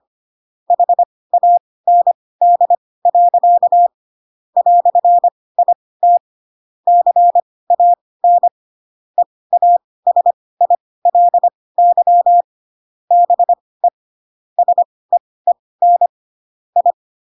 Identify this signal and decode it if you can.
Morse code, heard as a keyed tone, about 25 wpm, spaced as for 18 wpm, 700 Hz. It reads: HAND. "IT CAN EASILY BE SEEN I
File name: Pinocchio / Pinocchio ch1 12 25 ik1hge